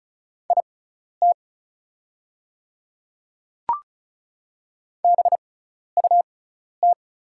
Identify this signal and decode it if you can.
Morse code, heard as a keyed tone, about 36 wpm, spaced as for 12 wpm, 690 Hz. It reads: IT BUT
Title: 6x Character Spacing - Top 300 Words - 35wpm